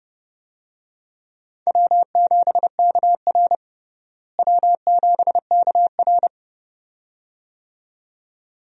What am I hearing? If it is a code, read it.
Morse code, heard as a keyed tone, about 31 wpm, 700 Hz. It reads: W7KR W7KR
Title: USA Call Signs - 4 Characters - Repeated 2x - 30wpm